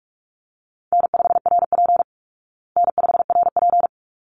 Morse code, heard as a keyed tone, about 44 wpm, 700 Hz. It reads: N5RP N5RP
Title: Easy Call Signs Repeated 2x - 45wpm